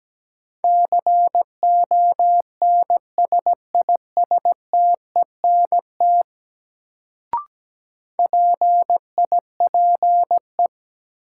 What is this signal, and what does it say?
Morse code, heard as a keyed tone, about 17 wpm, 700 Hz. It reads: CONSISTENT PIPE
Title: Sentences from Top 1500 Words - New Word Drill Alt - 17wpm